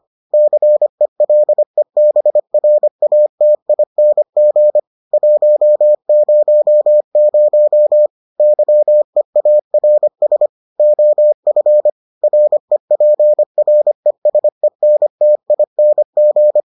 Morse code, heard as a keyed tone, about 25 wpm, 600 Hz. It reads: CELEBRATING 100 YEARS OF REPRESENTING